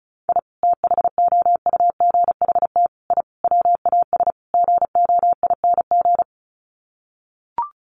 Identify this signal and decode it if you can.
Morse code, heard as a keyed tone, about 34 wpm, 700 Hz. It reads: I THOUGHT I WAS GOING
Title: Sets of 5 Words Encore2 35wpm